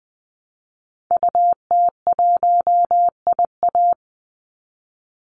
Morse code, heard as a keyed tone, about 20 wpm, 700 Hz. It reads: UT1IA
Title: Call Signs - 20wpm